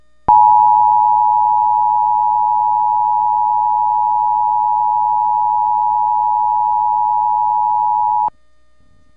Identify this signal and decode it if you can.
Emergency Alert System (SAME) tones (the 853 and 960 Hz attention tone). It is heard from 0.2 to 8.4 s.
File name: WHYY